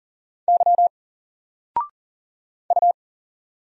Morse code, heard as a keyed tone, about 40 wpm, 700 Hz. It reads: Y U
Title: Single Character (Letter:Number:Punctuation:Prosign) Alt 40wpm